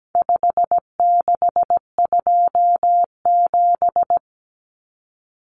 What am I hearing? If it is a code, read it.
Morse code, heard as a keyed tone, about 17 wpm, 700 Hz. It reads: 5627
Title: Common Cut Numbers - 4-Digit 17wpm